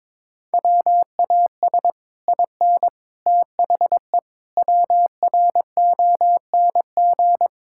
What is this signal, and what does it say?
Morse code, heard as a keyed tone, about 22 wpm, 700 Hz. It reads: WAS IN THE WRONG
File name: Sets of 9 Words - 22wpm